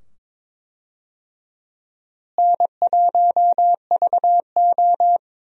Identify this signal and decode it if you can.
Morse code, heard as a keyed tone, about 22 wpm, 700 Hz. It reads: N1VO
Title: USA Call Signs - 4 Characters - Repeated 2x - 22wpm